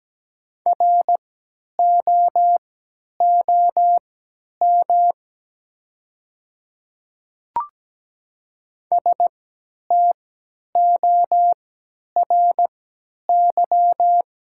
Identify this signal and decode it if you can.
Morse code, heard as a keyed tone, about 17 wpm, spaced as for 10 wpm, 700 Hz. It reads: ROOM STORY
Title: 3x Character Spacing - Top 500 Words - 17wpm